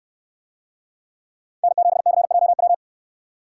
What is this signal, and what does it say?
Morse code, heard as a keyed tone, about 69 wpm, 700 Hz. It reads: N7CJW